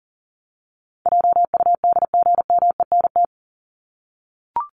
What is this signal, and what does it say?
Morse code, heard as a keyed tone, about 40 wpm, 700 Hz. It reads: JUDGMENT
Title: Top 1751-2000 Words 40wpm